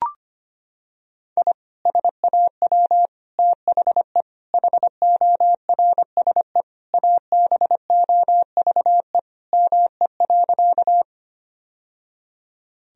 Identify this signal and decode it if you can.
Morse code, heard as a keyed tone, about 25 wpm, 700 Hz. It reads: I SAW THE HORSE ABOVE ME.
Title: Sentences from Top 1000 Words - Review Part 2 - 25wpm